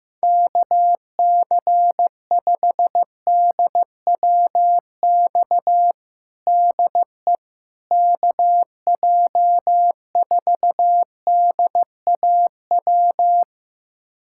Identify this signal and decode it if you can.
Morse code, heard as a keyed tone, about 16 wpm, 700 Hz. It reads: KC5DWX DE KJ4DAW
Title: USA Callsign DE Callsign - 6 Characters - 15wpm